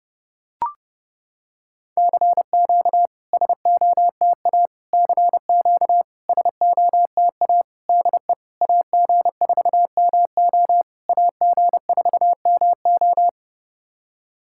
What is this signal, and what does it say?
Morse code, heard as a keyed tone, about 30 wpm, 700 Hz. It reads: CQ SOTA CQ SOTA DE AG4MO AG4MO
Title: SOTA QSO Practice 30wpm